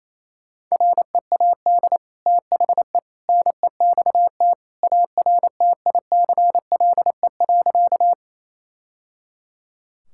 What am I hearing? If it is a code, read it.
Morse code, heard as a keyed tone, about 28 wpm, 700 Hz. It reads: READ THE NEXT ARTICLE.
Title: Sentences from Top 1000 Words - Review Part 5 - 28wpm